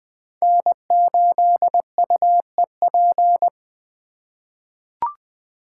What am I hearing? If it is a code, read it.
Morse code, heard as a keyed tone, about 20 wpm, 700 Hz. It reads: N8UEP
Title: Code Groups - 5 Letter:Numbers - 20wpm